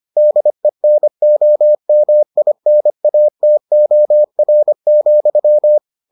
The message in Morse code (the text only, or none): DENOMINATOR,